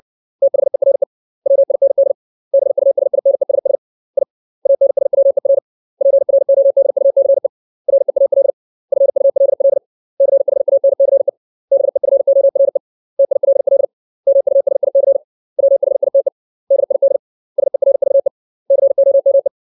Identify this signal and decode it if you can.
Morse code, heard as a keyed tone, about 60 wpm, 550 Hz. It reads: THERE WIND BUSINESS I ANSWER PRODUCE REAL FULL CHANCE BEFORE TILL MUSIC WHITE DID SERVE COME
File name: 632 words / 632 Words 60WPM